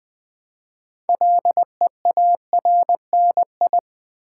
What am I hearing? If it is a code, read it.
Morse code, heard as a keyed tone, about 20 wpm, 700 Hz. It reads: LEARNI
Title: Build A Word - 2 Letters At A Time - 8-Letter Words - 20wpm